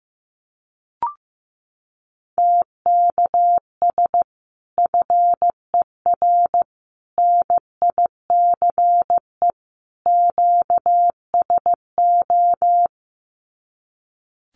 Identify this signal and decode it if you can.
Morse code, heard as a keyed tone, about 15 wpm, 700 Hz. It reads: TKS FER NICE QSO
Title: Common QSO Phrases 15wpm